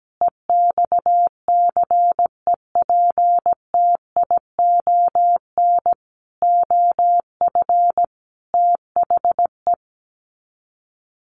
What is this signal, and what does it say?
Morse code, heard as a keyed tone, about 17 wpm, 700 Hz. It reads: EXCEPTION OF THE
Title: Sets of 5 Words Encore 17wpm